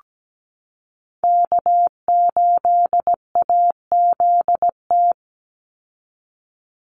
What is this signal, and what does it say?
Morse code, heard as a keyed tone, about 17 wpm, 700 Hz. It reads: K8AZT
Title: Call Signs Repeated 2x - 17wpm